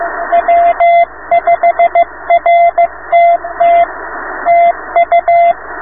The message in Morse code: W5RTT TU